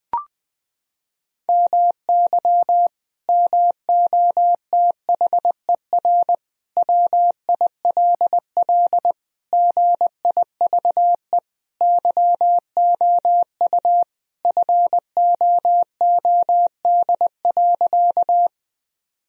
Morse code, heard as a keyed tone, about 20 wpm, 700 Hz. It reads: MY MOTHER WILL GIVE YOU FOOD.
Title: Sentences from Top 1000 Words - Review Part 1 - 20wpm